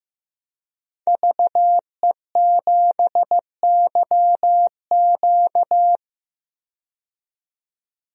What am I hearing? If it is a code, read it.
Morse code, heard as a keyed tone, about 15 wpm, 700 Hz. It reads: VE7YQ
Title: DXCC-100 Call Signs - 5 Characters - 15wpm